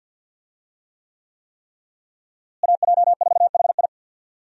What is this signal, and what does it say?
Morse code, heard as a keyed tone, about 49 wpm, 700 Hz. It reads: AJ4HI
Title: USA Call Signs - 5 Characters - Repeated 3x - 50wpm